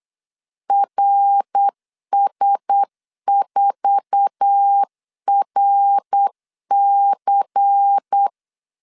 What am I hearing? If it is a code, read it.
Morse code, heard as a keyed tone, about 8 wpm, 800 Hz. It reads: RS4RC
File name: LEZIONI / 9 2 8wpm 3437